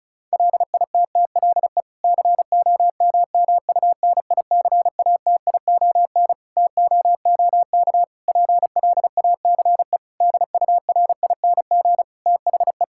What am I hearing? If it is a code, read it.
Morse code, heard as a keyed tone, about 34 wpm, 700 Hz. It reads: LITTLE COMMUNICATION TOOK PLACE DURING THE
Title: Sentences from Top 1000 Words - Review Part 6 - 35wpm